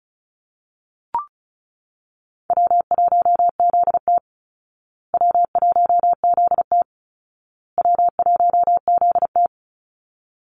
Morse code, heard as a keyed tone, about 34 wpm, 700 Hz. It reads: W1ZT W1ZT W1ZT
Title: Call Signs Repeated 2x - 35wpm